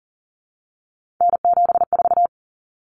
Morse code, heard as a keyed tone, about 40 wpm, 700 Hz. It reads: N74
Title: Common Cut Numbers - 3-Digit 40wpm